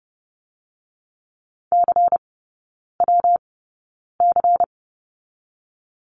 Morse code, heard as a keyed tone, about 30 wpm, 700 Hz. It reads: C W C
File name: Lesson 19e - Letter w - Three Characters 30wpm